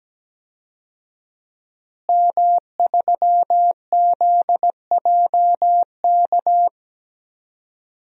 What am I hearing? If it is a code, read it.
Morse code, heard as a keyed tone, about 17 wpm, 700 Hz. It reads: M3ZJK
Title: DXCC-100 Call Signs - 5 Characters - Repeated 2x - 17wpm